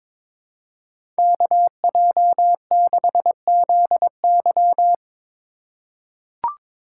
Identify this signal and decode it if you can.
Morse code, heard as a keyed tone, about 22 wpm, 700 Hz. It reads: KJ6ZY
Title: USA Call Signs - 5 Characters - 2x Character Spacing - 22wpm